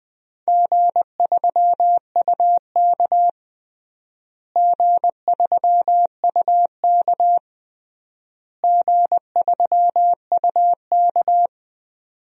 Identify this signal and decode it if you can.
Morse code, heard as a keyed tone, about 20 wpm, 700 Hz. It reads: G3UK G3UK G3UK
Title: DXCC-100 Call Signs - 4 Characters - Repeated 3x - 20wpm